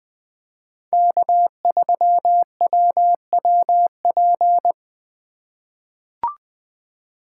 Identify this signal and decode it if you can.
Morse code, heard as a keyed tone, about 20 wpm, 700 Hz. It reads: K3WWP
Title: K1USN SST QSO Practice - 20wpm